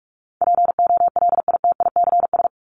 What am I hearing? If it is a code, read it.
Morse code, heard as a keyed tone, about 44 wpm, 700 Hz. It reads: POLITICS
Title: Top 751-1000 Words 45wpm